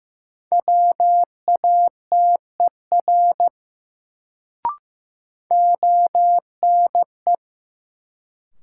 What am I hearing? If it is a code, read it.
Morse code, heard as a keyed tone, about 15 wpm, 700 Hz. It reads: WATER ONE
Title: Sentences from Top 100 Words - New Word Drill - 15wpm